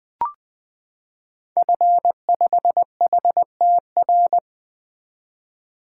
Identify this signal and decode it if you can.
Morse code, heard as a keyed tone, about 20 wpm, 700 Hz. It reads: F5HTR
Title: 2023 SOTA Activator Call Signs - 20wpm